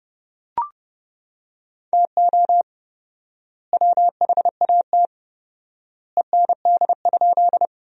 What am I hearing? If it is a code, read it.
Morse code, heard as a keyed tone, about 30 wpm, 700 Hz. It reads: TO WHAT END?
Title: 4x Word-Spacing - Sentences from Top 200 Words - 30wpm